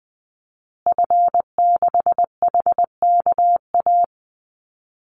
Morse code, heard as a keyed tone, about 20 wpm, 700 Hz. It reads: F6HKA
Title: Call Signs Repeated 2x - 20wpm